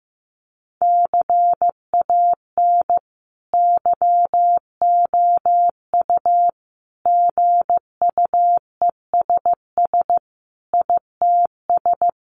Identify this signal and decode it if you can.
Morse code, heard as a keyed tone, about 15 wpm, 700 Hz. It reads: CAN YOU GUESS ITS
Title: Sentences from Top 1500 Words 15wpm